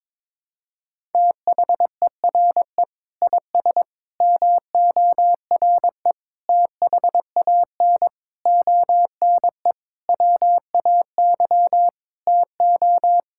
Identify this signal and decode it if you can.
Morse code, heard as a keyed tone, about 22 wpm, 700 Hz. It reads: THERE IS MORE THAN ONE WAY TO
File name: Sentences with 9-Word Phrases - 22wpm